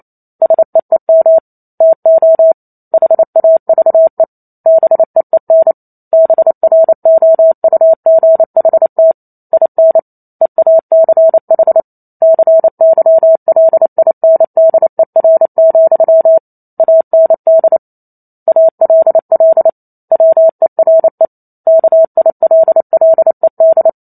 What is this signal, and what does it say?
Morse code, heard as a keyed tone, about 28 wpm, 650 Hz. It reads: SEEM TO HAVE BEEN BROUGHT IN EACH CYLINDER, AND ALL WERE KILLED